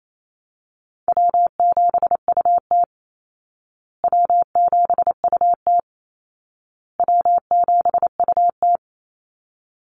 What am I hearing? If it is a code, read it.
Morse code, heard as a keyed tone, about 28 wpm, 700 Hz. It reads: W7UT W7UT W7UT
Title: Easy Call Signs Repeated 3x - 28wpm